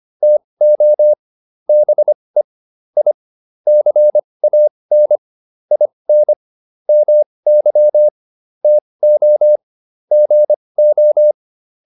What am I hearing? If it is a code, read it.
Morse code, heard as a keyed tone, about 25 wpm, spaced as for 20 wpm, 600 Hz. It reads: TO BE I CAN IN MY TO GO
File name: WD101a 20